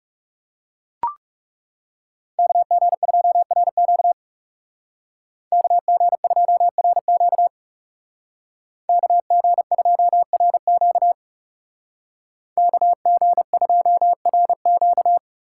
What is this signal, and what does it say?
Morse code, heard as a keyed tone, about 35 wpm, 700 Hz. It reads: KG2RQ KG2RQ KG2RQ KG2RQ
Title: Speed Racing - USA 2x2 Call Signs 30wpm